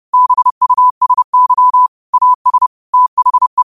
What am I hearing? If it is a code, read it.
Morse code, heard as a keyed tone, about 30 wpm, 1000 Hz. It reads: BUSY AS THE
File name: qotdfiles / 30wpm